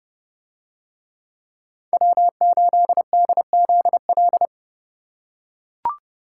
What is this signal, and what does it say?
Morse code, heard as a keyed tone, about 29 wpm, 700 Hz. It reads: W8DZL